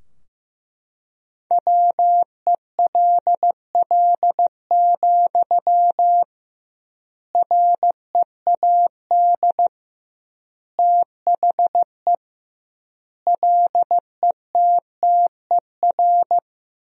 Morse code, heard as a keyed tone, about 15 wpm, 700 Hz. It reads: WELL, READ THE LETTER
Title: 2x Word-Spacing - Sentences from Top 300 Words - 15wpm